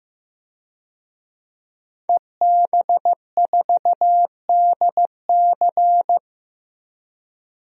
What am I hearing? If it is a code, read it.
Morse code, heard as a keyed tone, about 15 wpm, 700 Hz. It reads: EB4DC
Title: DXCC-100 Call Signs - 5 Characters - 15wpm